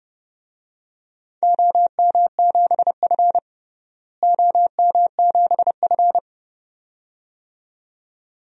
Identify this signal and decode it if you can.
Morse code, heard as a keyed tone, about 30 wpm, 700 Hz. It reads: OM7F OM7F
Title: DXCC-100 Call Signs - 4 Characters - Repeated 2x - 30wpm